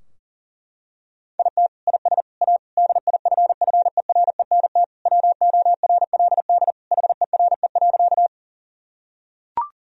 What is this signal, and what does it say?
Morse code, heard as a keyed tone, about 40 wpm, 700 Hz. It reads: IT IS A DIFFERENT WORLD HERE.
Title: Sentences from Top 1000 Words - Review Part 3 - 40wpm